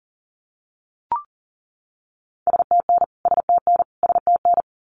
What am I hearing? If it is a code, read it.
Morse code, heard as a keyed tone, about 40 wpm, 700 Hz. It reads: STN STN STN
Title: Abbreviations Repeated 3x - 40wpm